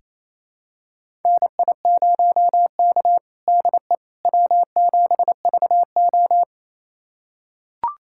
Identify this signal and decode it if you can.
Morse code, heard as a keyed tone, about 27 wpm, 690 Hz. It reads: NI0K DE W7VO
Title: USA Callsign DE Callsign - 4 Characters - 28wpm